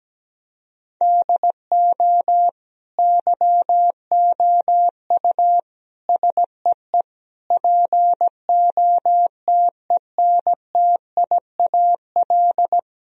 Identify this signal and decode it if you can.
Morse code, heard as a keyed tone, about 17 wpm, 700 Hz. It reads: DO YOU SEE POTENTIAL
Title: Sentences from Top 1000 Words - Review Part 4 - 17wpm